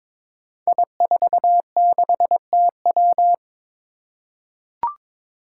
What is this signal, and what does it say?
Morse code, heard as a keyed tone, about 23 wpm, 700 Hz. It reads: I46TW
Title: Five Challenging Letter-Number Combinations 22wpm